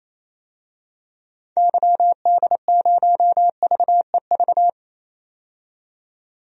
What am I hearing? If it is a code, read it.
Morse code, heard as a keyed tone, about 28 wpm, 700 Hz. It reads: YD0VEV
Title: DXCC-100 Call Signs - 6 Characters - 28wpm